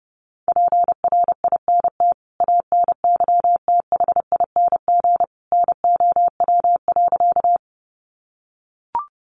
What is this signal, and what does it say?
Morse code, heard as a keyed tone, about 30 wpm, 700 Hz. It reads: PRINT ANYTHING NOW.
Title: Sentences from Top 1400 Words 30wpm